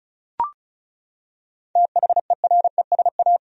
Morse code, heard as a keyed tone, about 36 wpm, 700 Hz. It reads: THERESA
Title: USA Common Names - 35wpm